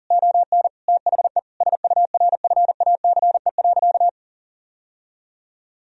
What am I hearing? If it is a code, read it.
Morse code, heard as a keyed tone, about 40 wpm, 700 Hz. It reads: ON THE SURFACE.
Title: Sentences from Top 1000 Words - Review Part 3 - 40wpm